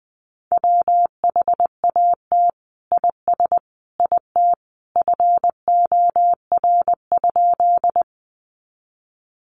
Morse code, heard as a keyed tone, about 20 wpm, 700 Hz. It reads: WHAT IS IT FOR?
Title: Sentences from Top 100 Words 20wpm